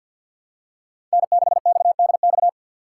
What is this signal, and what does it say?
Morse code, heard as a keyed tone, about 52 wpm, 700 Hz. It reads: N6XDX
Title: USA Call Signs - 2x Character Spacing - 50wpm